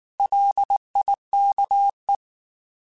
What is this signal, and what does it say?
Morse code, heard as a keyed tone, about 19 wpm, 800 Hz. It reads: LIKE
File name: Alphabetical / Alphabetical mp3s / Ls @18